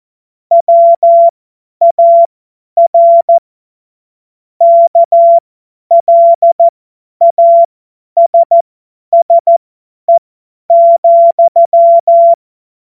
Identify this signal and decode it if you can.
Morse code, heard as a keyed tone, about 14 wpm, spaced as for 10 wpm, 680 Hz. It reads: WAR KLASSE,
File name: Morsekurs / Witze / Witz 21. März 2023 50 F70